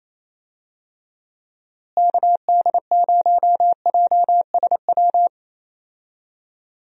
Morse code, heard as a keyed tone, about 28 wpm, 700 Hz. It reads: KD0JSW
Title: USA Call Signs - 6 Characters - Repeated 3x - 28wpm